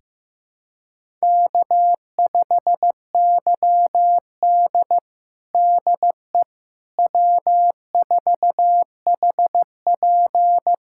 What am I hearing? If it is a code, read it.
Morse code, heard as a keyed tone, about 15 wpm, 700 Hz. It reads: K5YD DE W4HP
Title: USA Callsign DE Callsign - 4 Characters - 15wpm